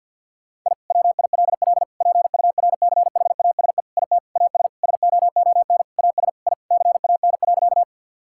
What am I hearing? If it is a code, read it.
Morse code, heard as a keyed tone, about 50 wpm, 700 Hz. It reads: I WILL PURCHASE IT AS SOON AS I CAN.
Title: Sentences from Top 1000 Words - Review Part 5 - 50wpm